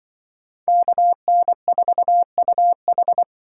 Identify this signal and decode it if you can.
Morse code, heard as a keyed tone, about 24 wpm, 700 Hz. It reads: KN4UH
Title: Speed Racing - USA Call Signs 20wpm